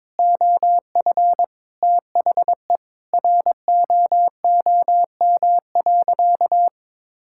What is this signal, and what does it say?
Morse code, heard as a keyed tone, about 22 wpm, 700 Hz. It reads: OF THE ROOM.
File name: practice / 22wpm / Sentences with 4-Word Phrases - 22wpm